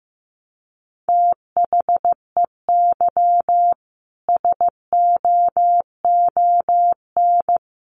Morse code, heard as a keyed tone, about 15 wpm, 700 Hz. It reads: THEY SOON
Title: Sentences from Top 1200 Words 15wpm